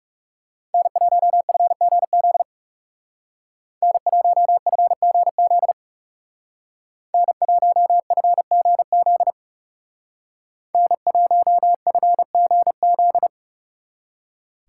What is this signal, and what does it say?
Morse code, heard as a keyed tone, about 37 wpm, 700 Hz. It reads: N1FGZ N1FGZ N1FGZ N1FGZ
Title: Speed Racing - USA Call Signs 30wpm